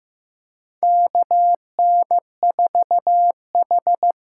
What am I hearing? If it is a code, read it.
Morse code, heard as a keyed tone, about 15 wpm, 700 Hz. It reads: KN4H
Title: USA Call Signs - 6 Characters - 8x Character Spacing - 15wpm